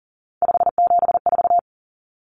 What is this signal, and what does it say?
Morse code, heard as a keyed tone, about 40 wpm, 700 Hz. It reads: 574
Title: Numbers - 3-Digit 40wpm